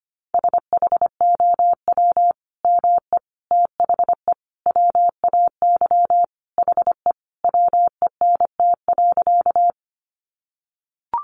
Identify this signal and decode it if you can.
Morse code, heard as a keyed tone, about 25 wpm, 700 Hz. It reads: SHOW ME THE WAY HE WENT.
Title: Sentences from Top 200 Words 25wpm